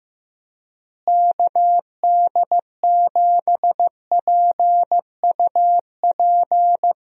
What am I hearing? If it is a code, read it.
Morse code, heard as a keyed tone, about 15 wpm, 700 Hz. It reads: KD7PUP
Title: USA Call Signs - 6 Characters - Repeated 2x - 15wpm